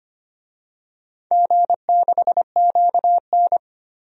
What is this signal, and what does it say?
Morse code, heard as a keyed tone, about 25 wpm, 700 Hz. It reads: G6QN
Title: DXCC-100 Call Signs - 4 Characters - 25wpm